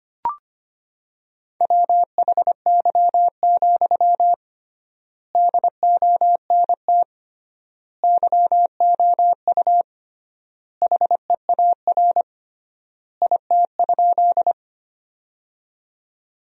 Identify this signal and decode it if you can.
Morse code, heard as a keyed tone, about 25 wpm, 700 Hz. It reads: WHY, DONT YOU HEAR IT?
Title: 3x Word-Spacing - Sentences from Top 300 Words - 25wpm